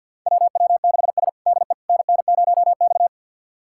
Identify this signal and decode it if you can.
Morse code, heard as a keyed tone, about 49 wpm, 700 Hz. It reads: WW6S DE NN0X
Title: USA Callsign DE Callsign - 4 Characters - 50wpm